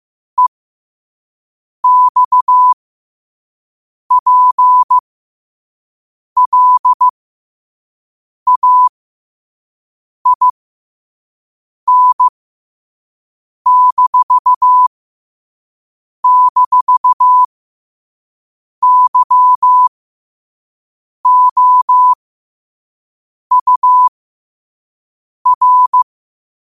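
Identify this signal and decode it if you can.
Morse code, heard as a keyed tone, about 15 wpm, spaced as for 5 wpm, 1000 Hz. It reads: EXPLAIN--YOUR
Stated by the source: Morse code quote-of-the-day practice at 5 wpm.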